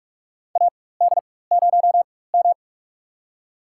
Morse code, heard as a keyed tone, about 47 wpm, spaced as for 21 wpm, 700 Hz. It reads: AD0M
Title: USA Call Signs - 4 Characters - 4x Character Spacing - 45wpm